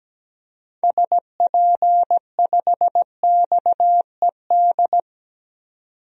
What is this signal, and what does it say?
Morse code, heard as a keyed tone, about 17 wpm, 700 Hz. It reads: SP5XED
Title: DXCC-100 Call Signs - 6 Characters - 17wpm